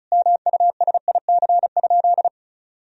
Morse code, heard as a keyed tone, about 36 wpm, 700 Hz. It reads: MUSIC?
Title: Sentences from Top 300 Words and Abbreviations 35wpm